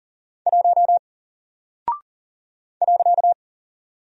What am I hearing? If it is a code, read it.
Morse code, heard as a keyed tone, about 40 wpm, 700 Hz. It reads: 1 .
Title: Single Character (Letter:Number:Punctuation:Prosign) Alt 40wpm